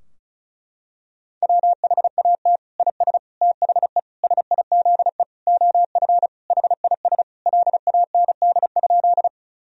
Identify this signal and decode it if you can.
Morse code, heard as a keyed tone, about 34 wpm, 700 Hz. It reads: WHAT IS THE SIZE OF HIS LAND?
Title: Sentences from Top 1000 Words - Review Part 2 - 35wpm